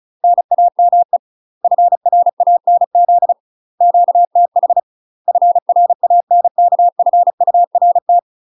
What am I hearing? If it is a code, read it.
Morse code, heard as a keyed tone, about 34 wpm, 700 Hz. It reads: NAME FRANZ QTH FRANKFURT
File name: UK MASTER QSOs TRAINING / 35WPM / UK MASTER QSOs 0001